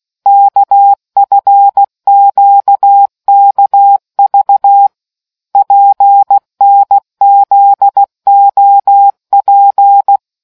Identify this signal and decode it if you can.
Morse code, heard as a keyed tone, about 16 wpm, 800 Hz. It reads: KFQKV PNZOP